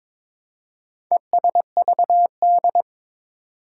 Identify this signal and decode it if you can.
Morse code, heard as a keyed tone, about 22 wpm, 700 Hz. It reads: ESVD
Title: Four Challenging Letter-Number Combinations 22wpm